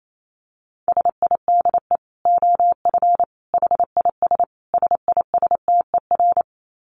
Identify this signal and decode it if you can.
Morse code, heard as a keyed tone, about 28 wpm, 700 Hz. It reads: SIDE OF HIS SISTER
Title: Dit Challenge 28wpm